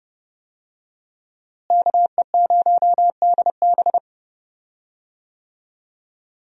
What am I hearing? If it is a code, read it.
Morse code, heard as a keyed tone, about 30 wpm, 700 Hz. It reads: KE0DB
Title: USA Call Signs - 5 Characters - Repeated 3x - 30wpm